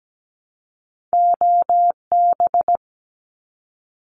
Morse code, heard as a keyed tone, about 17 wpm, 700 Hz. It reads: OB
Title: Abbreviations 17wpm